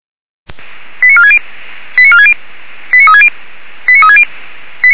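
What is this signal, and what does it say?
Five-tone selective call (ZVEI), heard as quick runs of five tones, tones of about 70 ms. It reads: 89460 89460 89360 89360
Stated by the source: Five-tone ZVEI fire brigade selective call received on an SDR, from the feuerwehr-orlamuende audio server.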